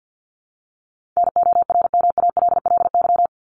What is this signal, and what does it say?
Morse code, heard as a keyed tone, about 50 wpm, 700 Hz. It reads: NORMALLY
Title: LY Suffix - Top 100 Words with Base Words 50wpm